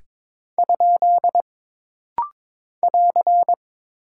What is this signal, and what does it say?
Morse code, heard as a keyed tone, about 22 wpm, 700 Hz. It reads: ? <AR>
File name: practice / 22wpm / Single Punctuation-Prosign - Rapid-Fire - 22wpm